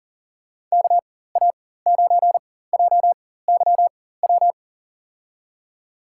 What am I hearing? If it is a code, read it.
Morse code, heard as a keyed tone, about 40 wpm, spaced as for 19 wpm, 700 Hz. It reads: KA9JYW
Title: USA Call Signs - 6 Characters - 4x Character Spacing - 40wpm